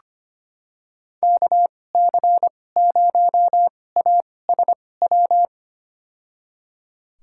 Morse code, heard as a keyed tone, about 25 wpm, spaced as for 18 wpm, 700 Hz. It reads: KC0ASW
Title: USA Call Signs - 6 Characters - 2x Character Spacing - 25wpm